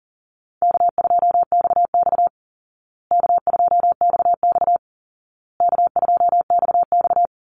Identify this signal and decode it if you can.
Morse code, heard as a keyed tone, about 40 wpm, 700 Hz. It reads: K2XX K2XX K2XX
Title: Call Signs Repeated 2x - 40wpm